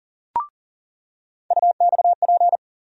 Morse code, heard as a keyed tone, about 40 wpm, 700 Hz. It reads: UXP
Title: USA Call Sign Postfixes - 3 Letters 40wpm